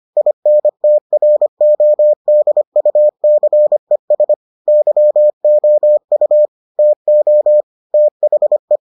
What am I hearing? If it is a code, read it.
Morse code, heard as a keyed tone, about 25 wpm, 600 Hz. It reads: INTRODUCES YOU TO THE